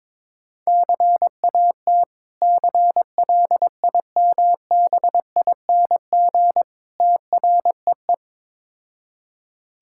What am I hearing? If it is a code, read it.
Morse code, heard as a keyed tone, about 22 wpm, 690 Hz. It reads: CAT CLIMBING TREE
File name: Difficult Words with Hints - 22wpm